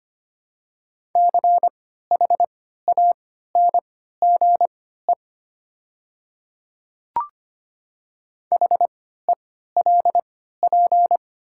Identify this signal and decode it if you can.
Morse code, heard as a keyed tone, about 25 wpm, spaced as for 14 wpm, 690 Hz. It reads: CHANGE HELP
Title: 3x Character Spacing - Top 200 Words - 25wpm